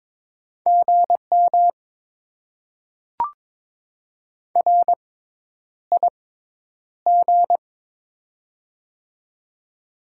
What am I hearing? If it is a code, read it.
Morse code, heard as a keyed tone, about 22 wpm, spaced as for 10 wpm, 700 Hz. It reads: GM RIG
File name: Top 20 QSO Elements - 6x Character Spacing - 22wpm